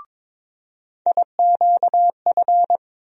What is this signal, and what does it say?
Morse code, heard as a keyed tone, about 22 wpm, 700 Hz. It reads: IQF
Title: USA Call Sign Postfixes - 3 Letters 22wpm